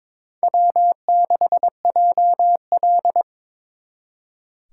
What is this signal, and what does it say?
Morse code, heard as a keyed tone, about 26 wpm, 700 Hz. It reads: W6JL
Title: Speed Racing - USA 1x2 Call Signs 22wpm